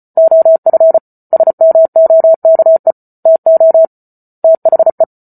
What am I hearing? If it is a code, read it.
Morse code, heard as a keyed tone, about 34 wpm, 650 Hz. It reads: OF SMOKE TO THE
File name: Book Two / 08-Chapter 8 - 34 WPM